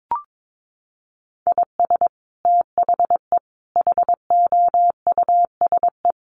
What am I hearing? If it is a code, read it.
Morse code, heard as a keyed tone, about 22 wpm, 700 Hz. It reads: IS THE HOUSE
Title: Sentences from Top 1400 Words 22wpm